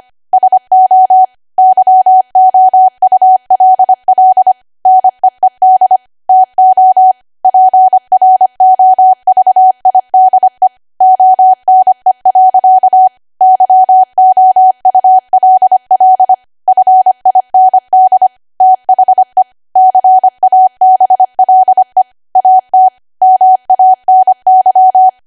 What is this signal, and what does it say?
Morse code, heard as a keyed tone, about 25 wpm, 750 Hz. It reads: SO YOULL NEED TO PROVIDE ONE. YOULL FIND THE CABLE AT MANY